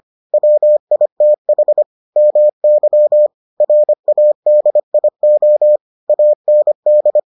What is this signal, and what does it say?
Morse code, heard as a keyed tone, about 25 wpm, 600 Hz. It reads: WITH MY RADIO AND